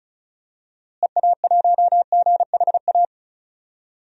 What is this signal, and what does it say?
Morse code, heard as a keyed tone, about 34 wpm, 700 Hz. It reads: EA1GHA
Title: DXCC-100 Call Signs - 6 Characters - Repeated 3x - 35wpm